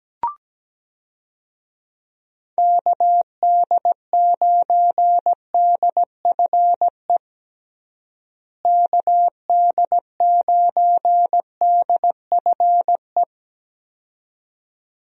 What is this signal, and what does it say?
Morse code, heard as a keyed tone, about 17 wpm, 700 Hz. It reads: KD9DFE KD9DFE
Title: USA Call Signs - 6 Characters - Repeated 2x - 17wpm